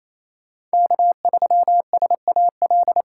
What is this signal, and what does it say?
Morse code, heard as a keyed tone, about 35 wpm, 700 Hz. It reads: K3SAL
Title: Speed Racing - USA 1x3 Call Signs 28wpm